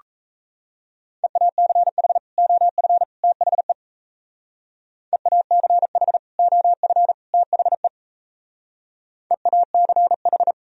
Morse code, heard as a keyed tone, about 36 wpm, 700 Hz. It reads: EACH OF THE EACH OF THE EACH